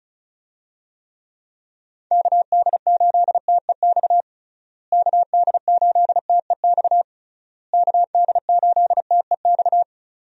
Morse code, heard as a keyed tone, about 36 wpm, 700 Hz. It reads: KD8TEX KD8TEX KD8TEX
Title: USA Call Signs - 6 Characters - Repeated 3x - 35wpm